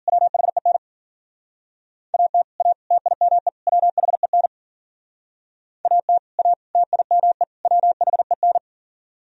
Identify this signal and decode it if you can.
Morse code, heard as a keyed tone, about 51 wpm, 700 Hz. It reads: WHEN AT A TIME WHEN AT A TIME WHEN